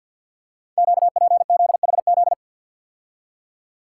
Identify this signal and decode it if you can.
Morse code, heard as a keyed tone, about 52 wpm, 700 Hz. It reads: QPZHZ
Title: Code Groups - 5 Letters - 50wpm